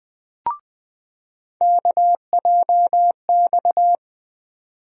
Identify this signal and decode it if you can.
Morse code, heard as a keyed tone, about 20 wpm, 700 Hz. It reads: KJX
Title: USA Call Sign Postfixes - 3 Letters 20wpm